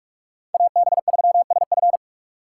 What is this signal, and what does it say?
Morse code, heard as a keyed tone, about 54 wpm, 690 Hz. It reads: AB3SF
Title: Speed Racing - USA 2x2 Call Signs 45wpm